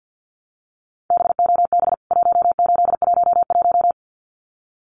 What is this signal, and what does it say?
Morse code, heard as a keyed tone, about 50 wpm, 700 Hz. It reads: BOB 1811
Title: CWT Exchange 50wpm